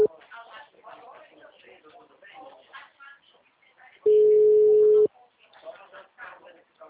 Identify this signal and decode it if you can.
Ringback tone (Europe, 425 Hz), with speech.